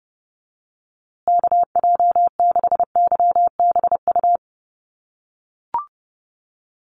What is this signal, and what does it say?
Morse code, heard as a keyed tone, about 30 wpm, 700 Hz. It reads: KJ6YBU
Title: USA Call Signs Repeated 2x 30wpm